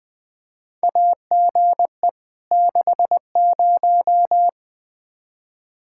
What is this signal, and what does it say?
Morse code, heard as a keyed tone, about 21 wpm, 700 Hz. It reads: AGE 60
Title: Practice QSO - Exchange 3 - More Info - 20wpm